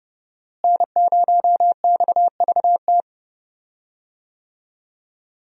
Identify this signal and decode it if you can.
Morse code, heard as a keyed tone, about 30 wpm, 700 Hz. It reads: N0XVT
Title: USA Call Signs - 5 Characters - Repeated 2x - 30wpm